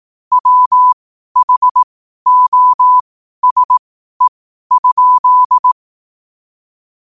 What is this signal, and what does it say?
Morse code, heard as a keyed tone, about 18 wpm, spaced as for 12 wpm, 1000 Hz. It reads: WHOSE?
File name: 12wpm